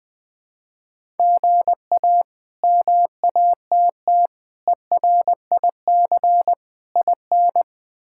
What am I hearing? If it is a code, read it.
Morse code, heard as a keyed tone, about 20 wpm, 700 Hz. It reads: GA MATT ERIC IN
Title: K1USN SST QSO Practice - 20wpm